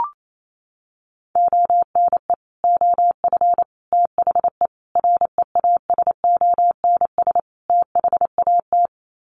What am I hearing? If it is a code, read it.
Morse code, heard as a keyed tone, about 28 wpm, 700 Hz. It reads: ONE OF THE REASONS THAT
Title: Sets of 5 Words Encore 28wpm